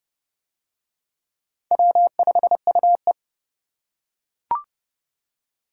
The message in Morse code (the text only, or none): W5UE